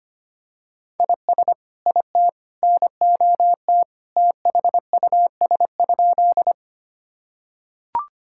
Morse code, heard as a keyed tone, about 25 wpm, 700 Hz. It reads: IS IT NOT THUS?
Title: Sentences from Top 1000 Words - Review Part 5 - 25wpm